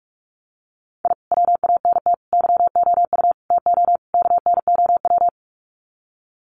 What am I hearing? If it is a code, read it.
Morse code, heard as a keyed tone, about 44 wpm, 700 Hz. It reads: I WANT YOU TO KNOW
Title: Sets of 5 Words Encore 45wpm